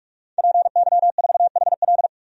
Morse code, heard as a keyed tone, about 46 wpm, 700 Hz. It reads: PY4HL
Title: DXCC-100 Call Signs - 5 Characters - 45wpm